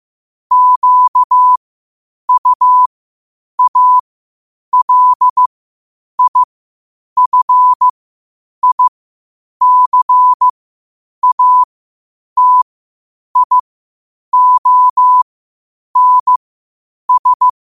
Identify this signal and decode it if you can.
Morse code, heard as a keyed tone, about 15 wpm, spaced as for 8 wpm, 1000 Hz. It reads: QUALIFICATIONS
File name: qotdfiles / 8wpm